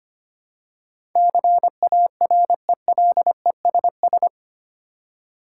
Morse code, heard as a keyed tone, about 25 wpm, 700 Hz. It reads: CARELESS
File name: Eight Letter Words - 25wpm